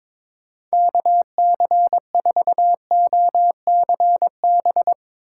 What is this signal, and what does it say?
Morse code, heard as a keyed tone, about 22 wpm, 700 Hz. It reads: KC4OCB
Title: USA Call Signs - 6 Characters - 3x Character Spacing - 22wpm